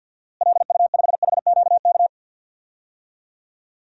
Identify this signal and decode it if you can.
Morse code, heard as a keyed tone, about 50 wpm, 700 Hz. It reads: LU5HQK
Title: DXCC-100 Call Signs - 6 Characters - Repeated 2x - 50wpm